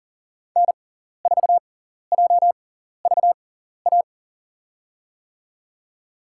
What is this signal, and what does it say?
Morse code, heard as a keyed tone, about 40 wpm, spaced as for 14 wpm, 700 Hz. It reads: N4JVA
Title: USA Call Signs - 5 Characters - 6x Character Spacing - 40wpm